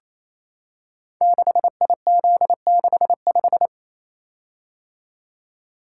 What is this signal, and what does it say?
Morse code, heard as a keyed tone, about 27 wpm, 700 Hz. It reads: 6IZ65
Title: Code Groups - 5 Letter:Numbers - 28wpm